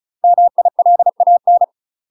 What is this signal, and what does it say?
Morse code, heard as a keyed tone, about 34 wpm, 700 Hz. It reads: MILAN
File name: UK MASTER QSOs TRAINING/35WPM